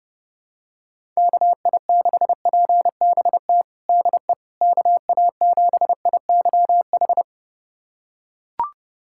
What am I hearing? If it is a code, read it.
Morse code, heard as a keyed tone, about 32 wpm, 700 Hz. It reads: KI6PBT DE KA7IYH
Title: USA Callsign DE Callsign - 6 Characters - 30wpm